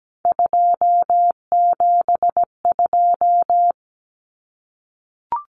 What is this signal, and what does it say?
Morse code, heard as a keyed tone, about 17 wpm, 700 Hz. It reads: 272
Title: Numbers - 3-Digit 17wpm